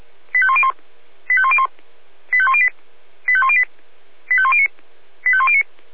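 Five-tone selective call (ZVEI), heard as quick runs of five tones, tones of about 70 ms. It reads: 85101 85101 85108 85108 85109 85109
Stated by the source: ZVEI five-tone selective call 85107, received by SDR.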